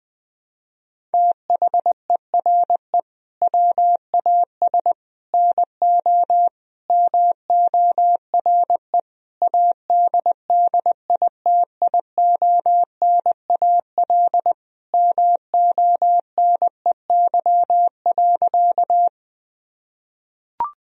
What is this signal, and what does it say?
Morse code, heard as a keyed tone, about 20 wpm, 700 Hz. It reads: THERE WAS NO MORE ADDITIONAL MONEY.
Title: Sentences from Top 1000 Words - Review Part 4 - 20wpm